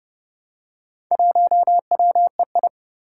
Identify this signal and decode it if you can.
Morse code, heard as a keyed tone, about 30 wpm, 700 Hz. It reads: 1WEI
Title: Four Challenging Letter-Number Combinations 30wpm